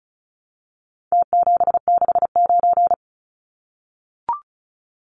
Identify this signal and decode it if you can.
Morse code, heard as a keyed tone, about 34 wpm, 700 Hz. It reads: T769
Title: Numbers - Contest Style - 4-Digit 35wpm